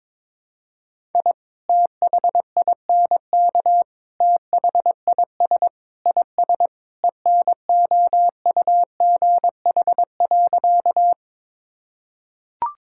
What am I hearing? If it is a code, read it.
Morse code, heard as a keyed tone, about 22 wpm, 700 Hz. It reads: I THINK THIS IS ENOUGH.
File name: practice / 22wpm / Sentences with 4-Word Phrases - 22wpm